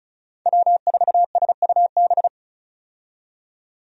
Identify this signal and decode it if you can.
Morse code, heard as a keyed tone, about 34 wpm, 700 Hz. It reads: W4SUB
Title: USA Call Signs - 5 Characters - 35wpm